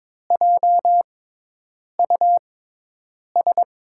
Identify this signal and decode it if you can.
Morse code, heard as a keyed tone, about 22 wpm, spaced as for 8 wpm, 700 Hz. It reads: JUS
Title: USA Call Signs - 6x Character Spacing - 22wpm